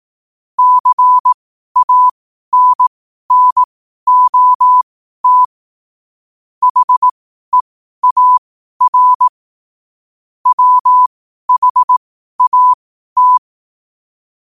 Morse code, heard as a keyed tone, about 18 wpm, spaced as for 12 wpm, 1000 Hz. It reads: CANNOT HEAR WHAT
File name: qotdfiles / 12wpm